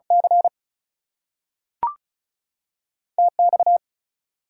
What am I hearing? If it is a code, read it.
Morse code, heard as a keyed tone, about 36 wpm, spaced as for 6 wpm, 700 Hz. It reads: CTX
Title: US States and Canadian Provinces 35wpm